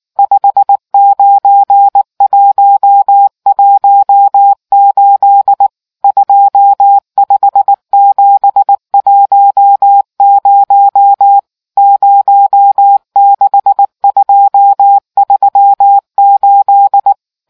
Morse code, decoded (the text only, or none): 59118 25710 06238